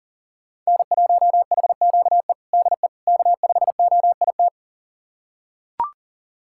Morse code, heard as a keyed tone, about 40 wpm, 700 Hz. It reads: N1HQE DE K5OIT
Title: USA Callsign DE Callsign - 5 Characters - 40wpm